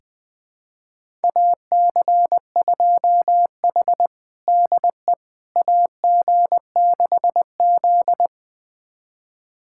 Morse code, heard as a keyed tone, about 20 wpm, 700 Hz. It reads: AC2H DE AG6Z
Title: USA Callsign DE Callsign - 4 Characters - 20wpm